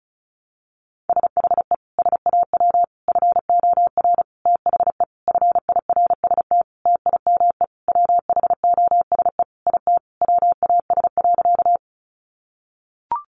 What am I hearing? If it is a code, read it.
Morse code, heard as a keyed tone, about 34 wpm, 700 Hz. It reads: SHE SAW FOR THE FIRST TIME WHOSE IT WAS.
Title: Sentences from Top 1200 Words 35wpm